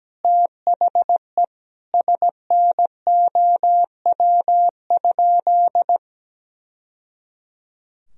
Morse code, heard as a keyed tone, about 17 wpm, 700 Hz. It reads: THE SNOW?
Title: Sentences from Top 1000 Words - Review Part 3 - 17wpm